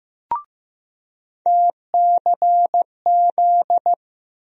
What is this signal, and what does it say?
Morse code, heard as a keyed tone, about 15 wpm, 700 Hz. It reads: TCZ
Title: USA Call Sign Postfixes - 3 Letters 15wpm